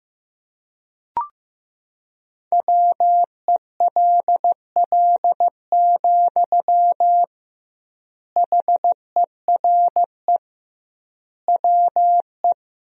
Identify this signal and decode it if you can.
Morse code, heard as a keyed tone, about 15 wpm, 700 Hz. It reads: WELL, HERE WE
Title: 2x Word-Spacing - Sentences from Top 300 Words - 15wpm